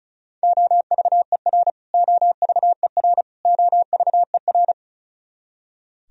Morse code, heard as a keyed tone, about 36 wpm, 700 Hz. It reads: OVER OVER OVER
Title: Top 100 Words Repeated 35wpm